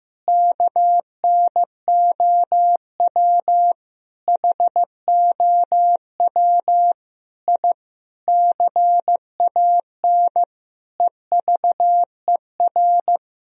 Morse code, heard as a keyed tone, about 15 wpm, 700 Hz. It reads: KNOW HOW I CAN EVER
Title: Sentences from Top 1000 Words - Review Part 2 - 15wpm